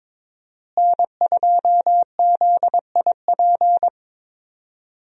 Morse code, heard as a keyed tone, about 22 wpm, 700 Hz. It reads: N2ZIP
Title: USA Call Signs - 5 Characters - Repeated 3x - 22wpm